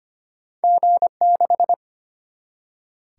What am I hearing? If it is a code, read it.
Morse code, heard as a keyed tone, about 25 wpm, 700 Hz. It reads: G6
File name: Two Letter-Number Combinations - 25wpm